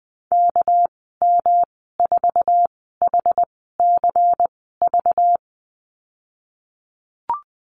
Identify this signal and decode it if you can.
Morse code, heard as a keyed tone, about 19 wpm, spaced as for 14 wpm, 700 Hz. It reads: KM4HCV
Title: 213 Lesson 34g - Number 8 - Call Signs 20wpm